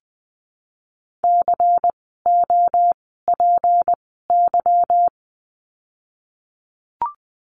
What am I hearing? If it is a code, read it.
Morse code, heard as a keyed tone, about 19 wpm, spaced as for 14 wpm, 700 Hz. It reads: COPY
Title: 153 Lesson 25f - Letter p - Words 20wpm